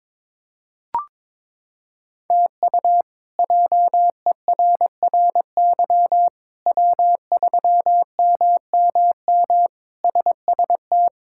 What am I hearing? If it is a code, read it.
Morse code, heard as a keyed tone, about 22 wpm, 700 Hz. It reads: TU JERRY W3MMM SST
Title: K1USN SST QSO Practice - 22wpm